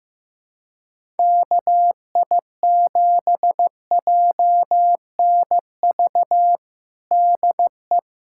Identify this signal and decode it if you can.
Morse code, heard as a keyed tone, about 15 wpm, 700 Hz. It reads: KI7JNV DE
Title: USA Callsign DE Callsign - 6 Characters - 15wpm